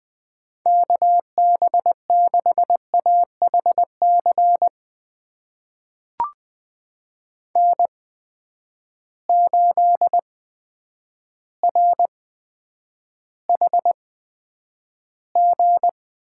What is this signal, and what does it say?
Morse code, heard as a keyed tone, about 20 wpm, spaced as for 9 wpm, 700 Hz. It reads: KB6AHC N8RHG
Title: USA Call Signs - 8x Character Spacing - 20wpm